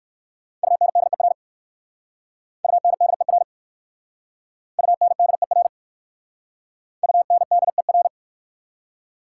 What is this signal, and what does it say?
Morse code, heard as a keyed tone, about 56 wpm, 690 Hz. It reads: UNDER UNDER UNDER UNDER
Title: Speed-Racing Prefixes and Suffixes 45wpm